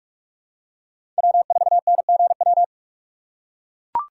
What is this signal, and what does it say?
Morse code, heard as a keyed tone, about 44 wpm, 700 Hz. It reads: W4NGW
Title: USA Call Signs - 5 Characters - 6x Character Spacing - 45wpm